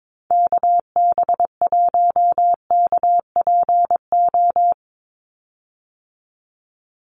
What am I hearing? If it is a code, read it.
Morse code, heard as a keyed tone, about 22 wpm, 700 Hz. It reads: KB1KPO
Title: USA Call Signs 22wpm